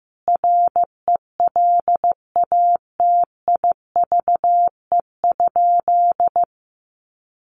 Morse code, heard as a keyed tone, about 15 wpm, 700 Hz. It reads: RELATIVE?
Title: Sentences from Top 1400 Words 15wpm